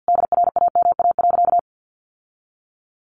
Morse code, heard as a keyed tone, about 50 wpm, 700 Hz. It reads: DRAMA.
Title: Sentences from Top 1500 Words 50wpm